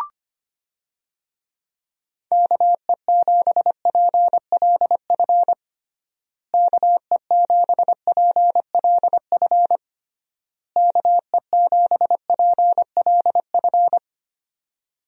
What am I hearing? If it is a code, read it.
Morse code, heard as a keyed tone, about 25 wpm, 700 Hz. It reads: KE7PLF KE7PLF KE7PLF
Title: USA Call Signs - 6 Characters - Repeated 3x - 25wpm